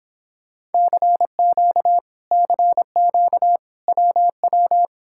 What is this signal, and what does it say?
Morse code, heard as a keyed tone, about 31 wpm, 700 Hz. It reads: CQ CQ WW